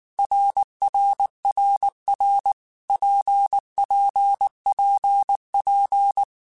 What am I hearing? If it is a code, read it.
Morse code, heard as a keyed tone, about 19 wpm, 800 Hz. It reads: RRRR PPPP
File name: Basic Letter Groups By7Groups / RKPX / RKPX@18